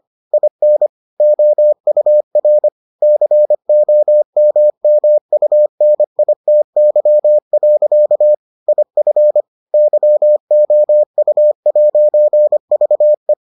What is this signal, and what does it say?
Morse code, heard as a keyed tone, about 25 wpm, 600 Hz. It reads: IN OUR COMMUNITY. IF YOU'VE